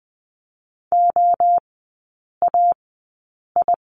Morse code, heard as a keyed tone, about 20 wpm, spaced as for 8 wpm, 700 Hz. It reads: OAI
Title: 041 Lesson 8e - Number 1 - Three Characters 20wpm